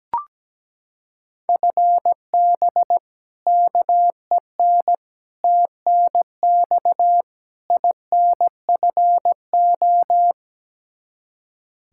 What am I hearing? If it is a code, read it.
Morse code, heard as a keyed tone, about 17 wpm, 700 Hz. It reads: FB KEN TNX INFO
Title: Practice QSO - Exchange 3 - More Info - 17wpm